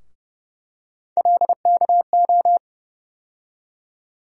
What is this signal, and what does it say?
Morse code, heard as a keyed tone, about 31 wpm, 700 Hz. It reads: LKO
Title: USA Call Sign Postfixes - 3 Letters 30wpm